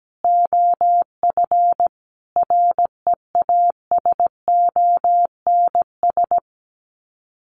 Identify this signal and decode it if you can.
Morse code, heard as a keyed tone, about 17 wpm, 700 Hz. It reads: OF REASONS
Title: Sets of 5 Words Encore2 17wpm